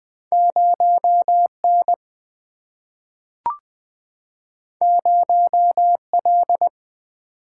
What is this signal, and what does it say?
Morse code, heard as a keyed tone, about 20 wpm, 700 Hz. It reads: 0 N 0 L
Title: Single-Letter Single-Number 20wpm